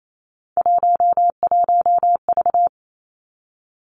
Morse code, heard as a keyed tone, about 28 wpm, 700 Hz. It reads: 11V
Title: Three Challenging Letter-Number Combinations 28wpm